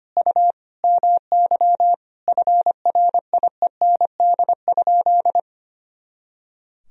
Morse code, heard as a keyed tone, about 25 wpm, 700 Hz. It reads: U MY FRIEND?
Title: Sentences from Top 300 Words and Abbreviations 25wpm